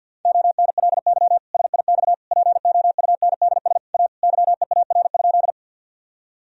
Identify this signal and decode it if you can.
Morse code, heard as a keyed tone, about 52 wpm, 700 Hz. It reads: ONLY SIX POUNDS A YEAR?
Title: Sentences from Top 1000 Words - Review Part 2 - 50wpm